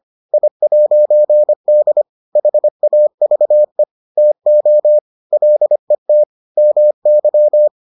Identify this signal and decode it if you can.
Morse code, heard as a keyed tone, about 25 wpm, 600 Hz. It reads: I'D HAVE TO LET MY